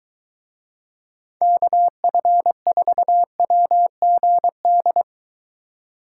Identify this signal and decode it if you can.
Morse code, heard as a keyed tone, about 20 wpm, 700 Hz. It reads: KF4WGD